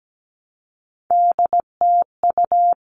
Morse code, heard as a keyed tone, about 17 wpm, 700 Hz. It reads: DTU
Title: All Cut Numbers - 3-Digit 17wpm